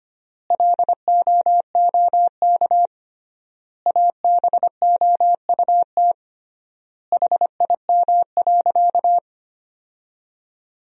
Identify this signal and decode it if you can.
Morse code, heard as a keyed tone, about 25 wpm, 700 Hz. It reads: LOOK ABOUT HIM.
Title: 3x Word-Spacing - Sentences from Top 200 Words - 25wpm